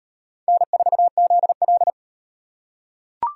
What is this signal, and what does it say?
Morse code, heard as a keyed tone, about 38 wpm, 700 Hz. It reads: N4ZL